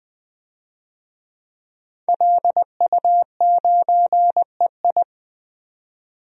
Morse code, heard as a keyed tone, about 20 wpm, 700 Hz. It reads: LU9EI
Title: DXCC-100 Call Signs - 5 Characters - 20wpm